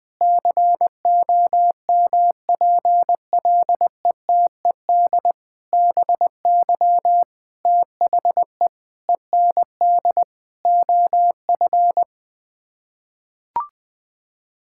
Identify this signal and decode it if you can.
Morse code, heard as a keyed tone, about 20 wpm, 700 Hz. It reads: COMPLETED BY THE END OF
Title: Sets of 8 Words - 20wpm